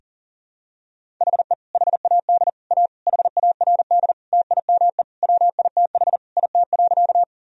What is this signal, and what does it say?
Morse code, heard as a keyed tone, about 40 wpm, 700 Hz. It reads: HE HAD A HARD TIME WITH IT.
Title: Sentences from Top 1000 Words - Review Part 1 - 40wpm